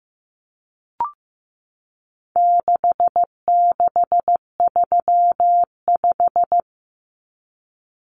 Morse code, heard as a keyed tone, about 15 wpm, 700 Hz. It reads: 6635
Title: Numbers - 4-Digit 15wpm